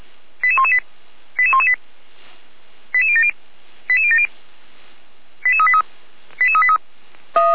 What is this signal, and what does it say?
Five-tone selective call (ZVEI), heard as quick runs of five tones, tones of about 70 ms. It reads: 88118 88118 88070 88070 88383 88383